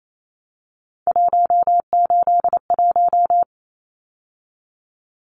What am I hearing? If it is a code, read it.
Morse code, heard as a keyed tone, about 28 wpm, 700 Hz. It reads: 181
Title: Numbers - 3-Digit 28wpm